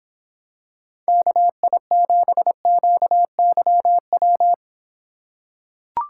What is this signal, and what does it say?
Morse code, heard as a keyed tone, about 26 wpm, 700 Hz. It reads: KI7QYW